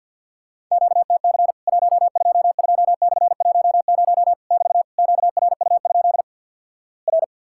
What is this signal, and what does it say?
Morse code, heard as a keyed tone, about 50 wpm, 700 Hz. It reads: QTC 122/10 <BT> QRU? R
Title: Worked All Europe DX Contest - QTC Exchange with Repeats - 50wpm